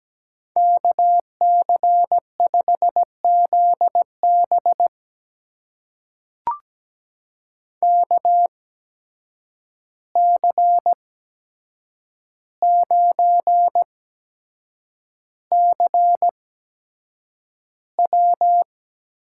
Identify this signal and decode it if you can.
Morse code, heard as a keyed tone, about 17 wpm, spaced as for 7 wpm, 700 Hz. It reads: KC5ZB KC9CW
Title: USA Call Signs - 5 Characters - 8x Character Spacing - 17wpm